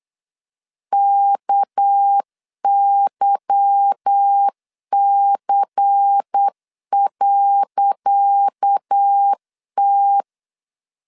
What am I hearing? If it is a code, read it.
Morse code, heard as a keyed tone, about 8 wpm, 800 Hz. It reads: KYC.T
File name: LEZIONI / 10 SIGNS / 10C DOT 8wpm 3649